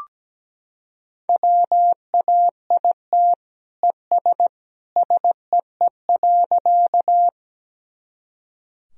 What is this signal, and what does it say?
Morse code, heard as a keyed tone, about 17 wpm, 700 Hz. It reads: WAIT ES SEE.
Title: Sentences from Top 500 Words and Abbreviations 17wpm